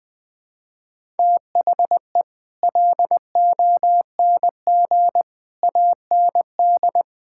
Morse code, heard as a keyed tone, about 20 wpm, 700 Hz. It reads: THE LONG AND
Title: Sentences with 9-Word Phrases - Spoken Beginning - 20wpm